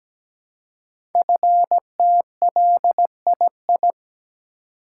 Morse code, heard as a keyed tone, about 17 wpm, 700 Hz. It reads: FTLII
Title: Code Groups - 5 Letters - 17wpm